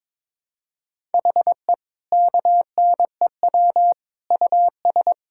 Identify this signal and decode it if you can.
Morse code, heard as a keyed tone, about 21 wpm, 700 Hz. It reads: HE KNEW US
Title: Sentences from Top 1000 Words - Review Part 5 - 22wpm